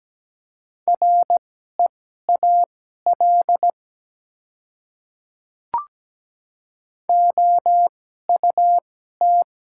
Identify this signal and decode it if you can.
Morse code, heard as a keyed tone, about 17 wpm, spaced as for 12 wpm, 700 Hz. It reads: REAL OUT
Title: 2x Character Spacing - Top 300 Words - 17wpm